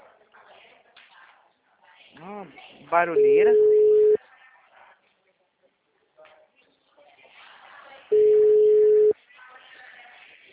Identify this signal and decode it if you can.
Ringback tone (Europe, 425 Hz), with speech.
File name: out- -2009- - - .33100